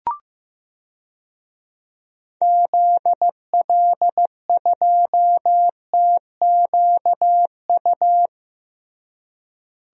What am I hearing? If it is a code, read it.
Morse code, heard as a keyed tone, about 15 wpm, 700 Hz. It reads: ZL2TQU
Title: DXCC-100 Call Signs - 6 Characters - Repeated 2x - 15wpm